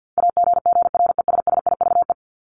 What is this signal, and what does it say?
Morse code, heard as a keyed tone, about 50 wpm, 700 Hz. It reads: AGGRESSIVE
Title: Top 1501-1750 Words 50wpm